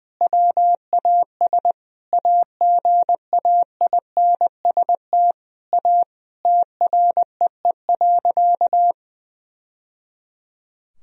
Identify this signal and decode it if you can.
Morse code, heard as a keyed tone, about 20 wpm, 700 Hz. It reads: WAS AGAINST A TREE.
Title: Sentences from Top 1000 Words - Review Part 1 - 20wpm